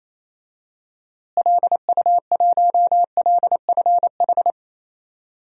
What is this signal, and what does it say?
Morse code, heard as a keyed tone, about 28 wpm, 700 Hz. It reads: LU1LFH
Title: DXCC-100 Call Signs - 6 Characters - Repeated 3x - 28wpm